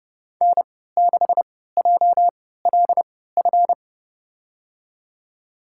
Morse code, heard as a keyed tone, about 32 wpm, spaced as for 17 wpm, 700 Hz. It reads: N6JLF
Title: USA Call Signs - 3x Character Spacing - 30wpm